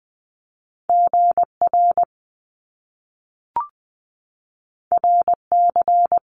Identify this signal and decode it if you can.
Morse code, heard as a keyed tone, about 20 wpm, 700 Hz. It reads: GR RC
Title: Top Two Letter Combinations in Words 20wpm